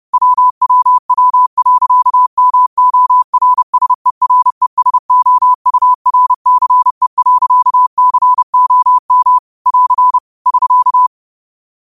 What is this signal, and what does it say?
Morse code, heard as a keyed tone, about 30 wpm, 1000 Hz. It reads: WWW.MORSERESOURCE.COM <AR> <SK>